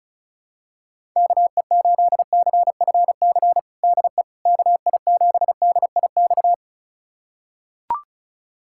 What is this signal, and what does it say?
Morse code, heard as a keyed tone, about 36 wpm, 700 Hz. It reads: KE8CFC DE KI7DIX
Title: USA Callsign DE Callsign - 6 Characters - 35wpm